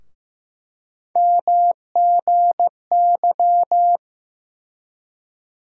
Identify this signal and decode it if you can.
Morse code, heard as a keyed tone, about 15 wpm, 700 Hz. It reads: MGY
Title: USA Call Sign Postfixes - 3 Letters 15wpm